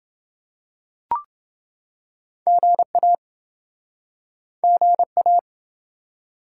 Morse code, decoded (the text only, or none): GA GA